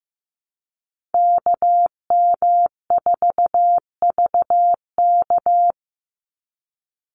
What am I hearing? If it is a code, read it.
Morse code, heard as a keyed tone, about 15 wpm, 700 Hz. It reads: KM4VK
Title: USA Call Signs 15wpm